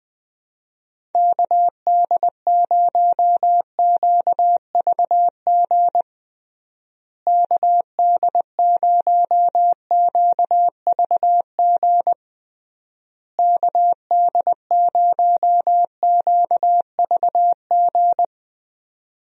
Morse code, heard as a keyed tone, about 20 wpm, 700 Hz. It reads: KD0QVG KD0QVG KD0QVG
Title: USA Call Signs - 6 Characters - Repeated 3x - 20wpm